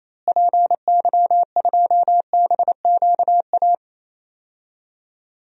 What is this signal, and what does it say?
Morse code, heard as a keyed tone, about 28 wpm, 700 Hz. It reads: PY2BQA
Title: DXCC-100 Call Signs - 6 Characters - Repeated 3x - 28wpm